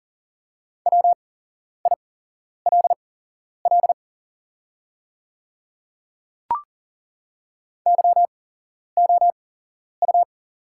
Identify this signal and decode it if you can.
Morse code, heard as a keyed tone, about 40 wpm, spaced as for 11 wpm, 690 Hz. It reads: WILL YOU
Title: 8x Character Spacing - Top 100 Words - 40wpm